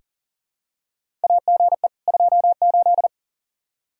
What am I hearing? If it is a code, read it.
Morse code, heard as a keyed tone, about 40 wpm, 700 Hz. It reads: AGE 28
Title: K1USN SST QSO Practice 40wpm